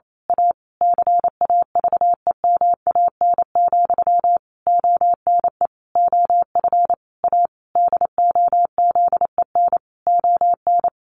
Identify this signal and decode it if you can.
Morse code, heard as a keyed tone, about 28 wpm, 700 Hz. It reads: A CAVEMAN, ONE OF A DOZEN ON
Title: A Sound Of Thunder 28wpm